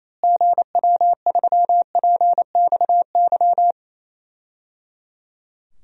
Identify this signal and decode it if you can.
Morse code, heard as a keyed tone, about 28 wpm, 700 Hz. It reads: GW3PXY
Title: DXCC-100 Call Signs - 6 Characters - Repeated 3x - 28wpm